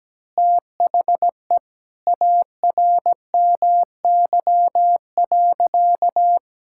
Morse code, heard as a keyed tone, about 17 wpm, 700 Hz. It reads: THE ARMY.
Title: Sentences from Top 1000 Words - Review Part 6 - 17wpm